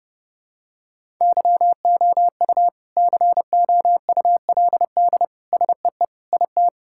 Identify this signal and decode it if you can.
Morse code, heard as a keyed tone, about 30 wpm, 700 Hz. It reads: YOU COULD SEE IT
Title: Sentences from Top 1000 Words - Review Part 5 - 30wpm